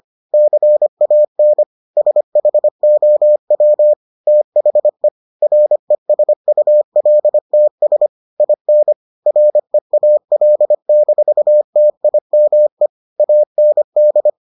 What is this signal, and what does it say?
Morse code, heard as a keyed tone, about 25 wpm, 600 Hz. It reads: CAN SHOW THE RESULTS IN REAL-TIME AND